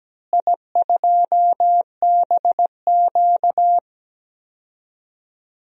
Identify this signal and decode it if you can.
Morse code, heard as a keyed tone, about 17 wpm, 700 Hz. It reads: I2BQ
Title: DXCC-100 Call Signs - 4 Characters - Repeated 3x - 17wpm